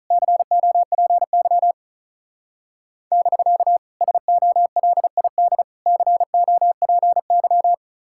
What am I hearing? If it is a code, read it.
Morse code, heard as a keyed tone, about 33 wpm, 700 Hz. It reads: COPY <BK> SOLID COPY
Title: Speed-Racing Common QSO Phrases - 35wpm